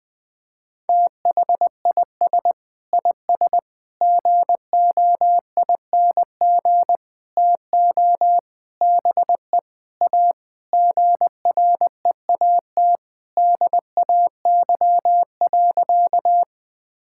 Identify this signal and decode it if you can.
Morse code, heard as a keyed tone, about 20 wpm, 700 Hz. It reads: THIS IS GOING TO BE A GREAT DAY.
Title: Sentences with 6-Word Phrases - 20wpm